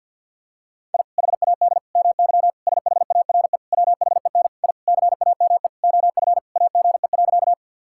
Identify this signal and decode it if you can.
Morse code, heard as a keyed tone, about 50 wpm, 700 Hz. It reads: I HAD MY SHARE WHEN I CAME OF AGE.
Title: Sentences from Top 1000 Words - Review Part 6 - 50wpm